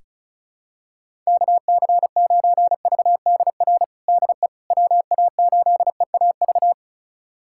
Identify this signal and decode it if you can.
Morse code, heard as a keyed tone, about 36 wpm, 700 Hz. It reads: KC9VDR DE WA8EAV
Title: USA Callsign DE Callsign 35wpm